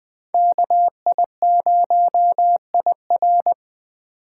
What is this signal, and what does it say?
Morse code, heard as a keyed tone, about 20 wpm, 700 Hz. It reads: KI0IR
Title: Speed Racing - USA 2x2 Call Signs 17wpm